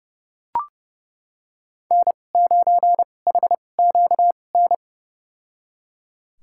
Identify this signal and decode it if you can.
Morse code, heard as a keyed tone, about 32 wpm, spaced as for 22 wpm, 700 Hz. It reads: N9HQN
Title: USA Call Signs - 2x Character Spacing - 30wpm